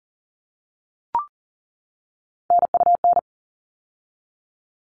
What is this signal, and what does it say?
Morse code, heard as a keyed tone, about 40 wpm, 700 Hz. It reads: NUN
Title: All Cut Numbers - 3-Digit 40wpm